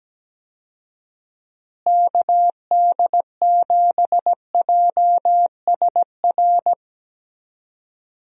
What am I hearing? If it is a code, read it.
Morse code, heard as a keyed tone, about 17 wpm, 700 Hz. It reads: KD7JSR
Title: USA Call Signs - 6 Characters - Repeated 3x - 17wpm